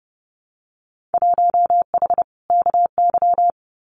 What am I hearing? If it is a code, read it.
Morse code, heard as a keyed tone, about 31 wpm, 700 Hz. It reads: 1H KY
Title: Winter Field Day Exchanges 30wpm